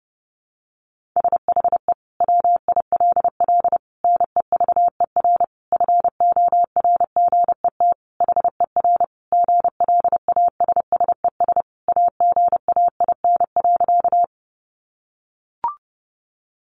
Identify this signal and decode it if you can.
Morse code, heard as a keyed tone, about 30 wpm, 700 Hz. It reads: SHE WILL NEVER FORGET HER GLASSES AGAIN.
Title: Additional Easy Phrases 30wpm